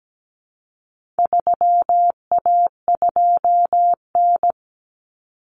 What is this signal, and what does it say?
Morse code, heard as a keyed tone, about 17 wpm, 700 Hz. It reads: 3A2N
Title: Common Cut Numbers - 4-Digit 17wpm